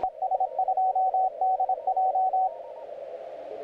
Morse code, heard as a keyed tone, about 25 wpm, 750 Hz. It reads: ES2DJ
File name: audio mp3 files / YK9G 40m CW 2008-4-12 1-21-51